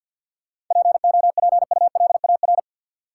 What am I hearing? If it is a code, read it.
Morse code, heard as a keyed tone, about 48 wpm, 700 Hz. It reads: POPULAR
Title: Sentences from Top 1200 Words - New Word Drill Alt - 50wpm